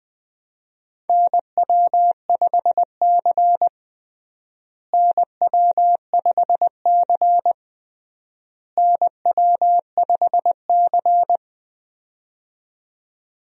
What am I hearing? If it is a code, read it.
Morse code, heard as a keyed tone, about 21 wpm, 700 Hz. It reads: NW5C NW5C NW5C
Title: USA Call Signs - 4 Characters - Repeated 3x - 20wpm